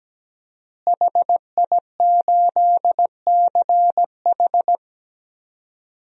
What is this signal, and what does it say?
Morse code, heard as a keyed tone, about 17 wpm, 700 Hz. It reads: HI8CH
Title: DXCC-100 Call Signs - 5 Characters - Repeated 3x - 17wpm